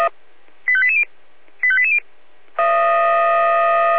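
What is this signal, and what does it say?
Five-tone selective call (ZVEI), heard as quick runs of five tones, tones of about 70 ms. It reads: 85880 85880
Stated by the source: ZVEI five-tone selective call received on SDR.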